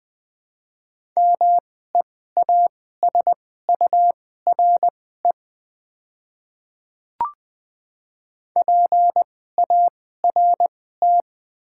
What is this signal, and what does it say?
Morse code, heard as a keyed tone, about 20 wpm, spaced as for 14 wpm, 690 Hz. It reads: MEASURE PART
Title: 2x Character Spacing - Top 500 Words - 20wpm